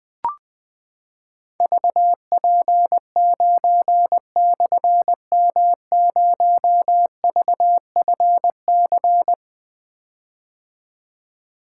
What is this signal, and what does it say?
Morse code, heard as a keyed tone, about 21 wpm, 700 Hz. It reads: VP9/M0VFC
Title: Difficult Call Signs 20wpm